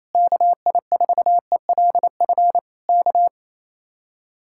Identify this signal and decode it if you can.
Morse code, heard as a keyed tone, about 26 wpm, 700 Hz. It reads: KI4ELF K
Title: Speed-Racing Common QSO Phrases - 28wpm